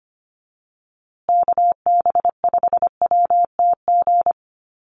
Morse code, heard as a keyed tone, about 25 wpm, 700 Hz. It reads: KB5WTG
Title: USA Call Signs Repeated 3x 25wpm